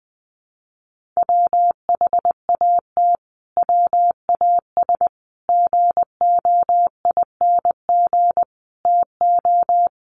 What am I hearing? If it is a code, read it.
Morse code, heard as a keyed tone, about 20 wpm, 700 Hz. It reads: WHAT WAS GOING TO
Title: Sets of 5 Words Encore 20wpm